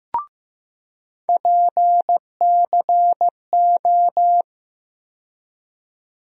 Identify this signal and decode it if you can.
Morse code, heard as a keyed tone, about 15 wpm, 700 Hz. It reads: PCO
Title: USA Call Sign Postfixes - 3 Letters 15wpm